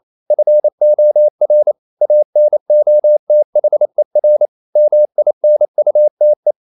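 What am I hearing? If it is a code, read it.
Morse code, heard as a keyed tone, about 28 wpm, 600 Hz. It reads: FOR ANOTHER MINUTE
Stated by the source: Morse code practice at 28 wpm, text of Ken Follett's novel Never in English, from CW med Gnister.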